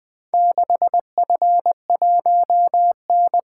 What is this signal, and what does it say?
Morse code, heard as a keyed tone, about 20 wpm, 700 Hz. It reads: 6F1N
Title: Four Letter-Number Combinations - 20wpm